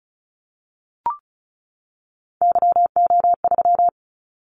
Morse code, heard as a keyed tone, about 34 wpm, 700 Hz. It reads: YO3
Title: Callsign Prefixes for Top 25 DXCC - 35wpm